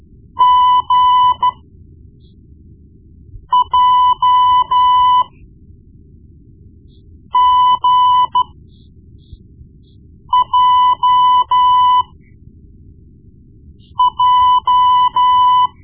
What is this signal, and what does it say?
Morse code, heard as a keyed tone, about 13 wpm, spaced as for 4 wpm, 1000 Hz. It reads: GJGJJ
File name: Morse Lessons / Morse Lessons / Tape-1B